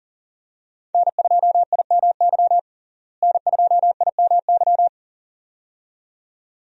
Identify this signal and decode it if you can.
Morse code, heard as a keyed tone, about 38 wpm, 700 Hz. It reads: N2IMY N2IMY
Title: USA Call Signs - 5 Characters - Repeated 2x - 40wpm